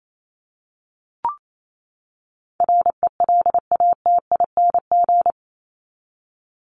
Morse code, heard as a keyed tone, about 28 wpm, 700 Hz. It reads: RELATING
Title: ING Suffix - Top 100 Words with Base Words 28wpm